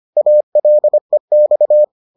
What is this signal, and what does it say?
Morse code, heard as a keyed tone, about 25 wpm, 600 Hz. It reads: ALEX